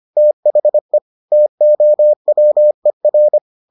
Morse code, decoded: THE TOWER

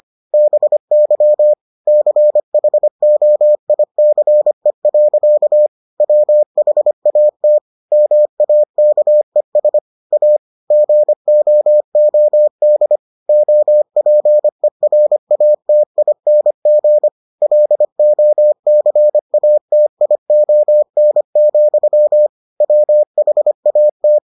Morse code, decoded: BY CHOICE. WHAT MAKES A GOOD OPERATING LOCATION, WHAT

